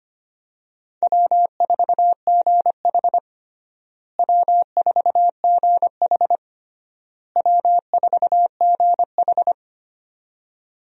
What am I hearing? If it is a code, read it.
Morse code, heard as a keyed tone, about 25 wpm, 700 Hz. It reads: W4GH W4GH W4GH
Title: USA Call Signs - 4 Characters - Repeated 3x - 25wpm